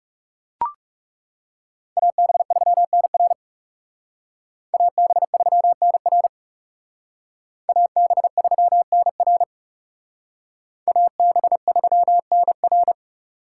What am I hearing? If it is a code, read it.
Morse code, heard as a keyed tone, about 37 wpm, 700 Hz. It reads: AB3NR AB3NR AB3NR AB3NR
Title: Speed Racing - USA Call Signs 30wpm